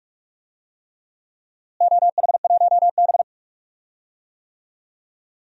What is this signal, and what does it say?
Morse code, heard as a keyed tone, about 44 wpm, 700 Hz. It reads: OH1B
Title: DXCC-100 Call Signs - 4 Characters - Repeated 3x - 45wpm